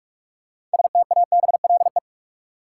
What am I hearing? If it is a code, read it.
Morse code, heard as a keyed tone, about 46 wpm, 700 Hz. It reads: STABLE
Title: Six Letter Words - 45wpm